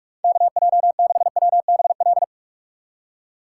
Morse code, heard as a keyed tone, about 44 wpm, 700 Hz. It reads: KJ6WBL